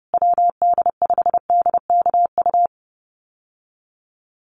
Morse code, heard as a keyed tone, about 30 wpm, 700 Hz. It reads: WD5DKU
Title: USA Call Signs Repeated 3x 30wpm